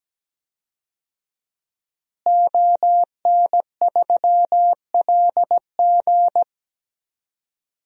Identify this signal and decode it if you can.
Morse code, heard as a keyed tone, about 17 wpm, 700 Hz. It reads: ON3LG
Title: DXCC-100 Call Signs - 5 Characters - Repeated 2x - 17wpm